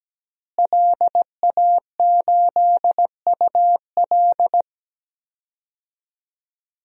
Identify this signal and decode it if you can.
Morse code, heard as a keyed tone, about 17 wpm, 700 Hz. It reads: LA8UL
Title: DXCC-100 Call Signs - 5 Characters - Repeated 2x - 17wpm